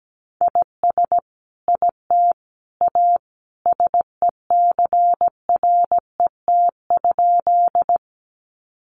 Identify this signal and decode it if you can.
Morse code, heard as a keyed tone, about 17 wpm, 700 Hz. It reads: IS IT A SECRET?
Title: Sentences from Top 1200 Words 17wpm